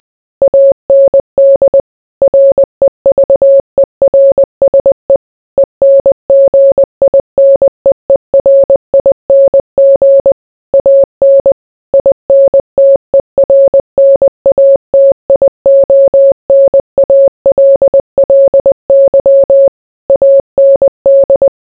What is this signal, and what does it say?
Morse code, heard as a keyed tone, about 20 wpm, 550 Hz. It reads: AND REVERSE ENGINEERING AN INTERNATIONALLY AND